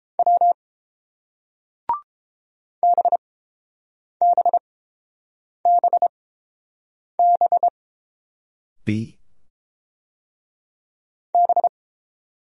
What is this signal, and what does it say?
Morse code, heard as a keyed tone, about 29 wpm, spaced as for 7 wpm, 700 Hz. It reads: W BBBB B